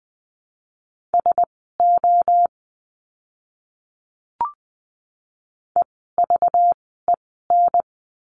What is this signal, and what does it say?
Morse code, heard as a keyed tone, about 19 wpm, spaced as for 14 wpm, 700 Hz. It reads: SO EVEN
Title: 230 Lesson 37f - Letter x - Words 20wpm